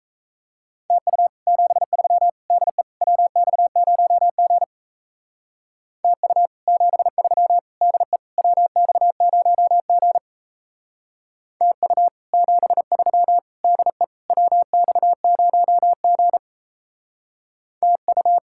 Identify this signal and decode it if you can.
Morse code, heard as a keyed tone, about 33 wpm, 700 Hz. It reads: TU 73 DE WX0G TU 73 DE WX0G TU 73 DE WX0G TU
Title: Speed-Racing Common QSO Phrases - 28wpm